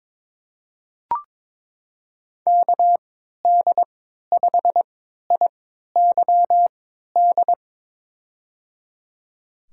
Morse code, heard as a keyed tone, about 22 wpm, spaced as for 13 wpm, 700 Hz. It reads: KD5IYD
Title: USA Call Signs - 6 Characters - 3x Character Spacing - 22wpm